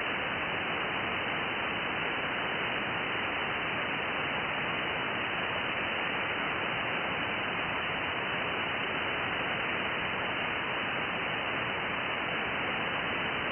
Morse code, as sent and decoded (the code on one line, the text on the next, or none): none